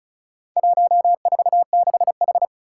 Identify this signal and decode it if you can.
Morse code, heard as a keyed tone, about 36 wpm, 700 Hz. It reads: 146H
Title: Four Challenging Letter-Number Combinations 35wpm